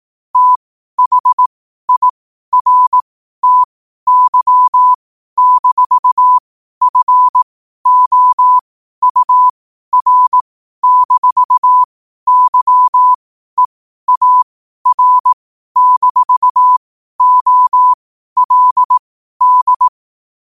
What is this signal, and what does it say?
Morse code, heard as a keyed tone, about 18 wpm, spaced as for 12 wpm, 1000 Hz. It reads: THIRTY-FOUR-YEAR-OLD
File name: qotdfiles / 12wpm